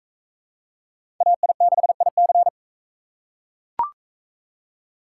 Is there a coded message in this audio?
Yes: Morse code from 1.2 to 2.5 s, heard as a keyed tone, about 37 wpm, 700 Hz. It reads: AI6IC